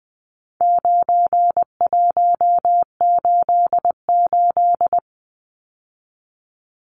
Morse code, heard as a keyed tone, about 20 wpm, 700 Hz. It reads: 9188
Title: Numbers - 4-Digit 20wpm